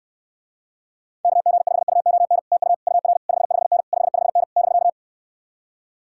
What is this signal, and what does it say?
Morse code, heard as a keyed tone, about 69 wpm, 700 Hz. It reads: KG5UON TU UR 55N 55N <BK>